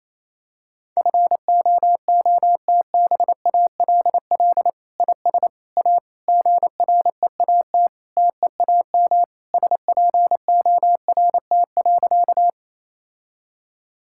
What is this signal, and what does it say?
Morse code, heard as a keyed tone, about 28 wpm, 700 Hz. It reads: FOOTBALL IS A GREAT TEAM SPORT.
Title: Sentences from Top 1600 Words 28wpm